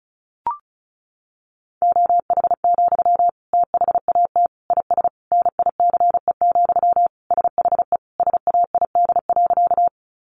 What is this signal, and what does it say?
Morse code, heard as a keyed tone, about 34 wpm, 700 Hz. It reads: OH, THAT IS NICE, SHE SAID.
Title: Sentences from Top 1100 Words 35wpm